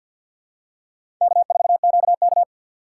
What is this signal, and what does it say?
Morse code, heard as a keyed tone, about 52 wpm, 700 Hz. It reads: K4QK
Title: USA Call Signs - 6x Character Spacing - 50wpm